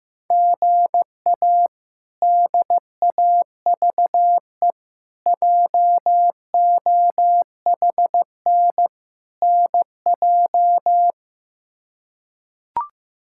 Morse code, heard as a keyed tone, about 15 wpm, 700 Hz. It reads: GA DAVE JOHN NJ
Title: K1USN SST QSO Practice - 15wpm